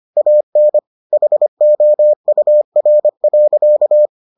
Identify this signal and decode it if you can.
Morse code, heard as a keyed tone, about 25 wpm, 600 Hz. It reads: AN HOUR.